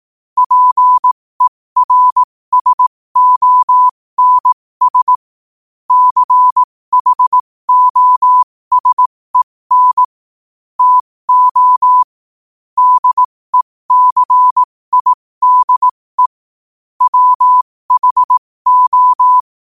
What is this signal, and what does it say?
Morse code, heard as a keyed tone, about 18 wpm, spaced as for 15 wpm, 1000 Hz. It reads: PERSONS CHOSEN TO DECIDE WHO
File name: qotdfiles / 15wpm